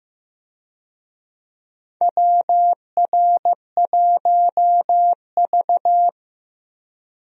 Morse code, heard as a keyed tone, about 15 wpm, 700 Hz. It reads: WR1V
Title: USA Call Signs - 4 Characters - Repeated 2x - 15wpm